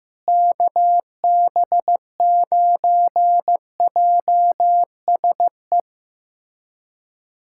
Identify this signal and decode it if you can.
Morse code, heard as a keyed tone, about 16 wpm, 700 Hz. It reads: KB9JSE
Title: USA Callsign DE Callsign 15wpm